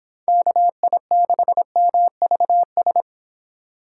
Morse code, heard as a keyed tone, about 26 wpm, 700 Hz. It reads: KI6MVS